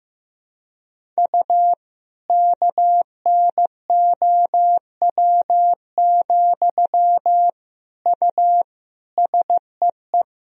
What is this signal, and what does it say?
Morse code, heard as a keyed tone, about 16 wpm, 700 Hz. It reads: U KNOW, U SEE
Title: Sentences from Top 200 Words and Abbreviations 15wpm